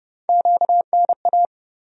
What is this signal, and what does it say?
Morse code, heard as a keyed tone, about 30 wpm, 700 Hz. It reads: QNA
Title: USA Call Sign Postfixes - 3 Letters 30wpm